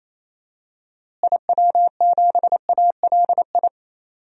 Morse code, heard as a keyed tone, about 28 wpm, 700 Hz. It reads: IW7ALI
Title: DXCC-100 Call Signs - 6 Characters - Repeated 3x - 28wpm